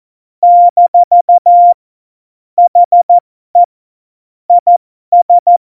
Morse code, heard as a keyed tone, about 14 wpm, spaced as for 12 wpm, 700 Hz. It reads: - HE IS